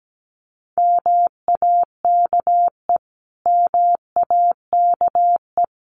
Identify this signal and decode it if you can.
Morse code, heard as a keyed tone, about 17 wpm, 700 Hz. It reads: MAKE MAKE
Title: Top 500 Words Repeated 17wpm